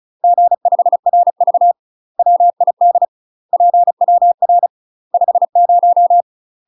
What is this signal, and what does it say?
Morse code, heard as a keyed tone, about 36 wpm, 700 Hz. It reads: G5RV WID PWR 50
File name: UK MASTER QSOs TRAINING / 35WPM / UK MASTER QSOs 0026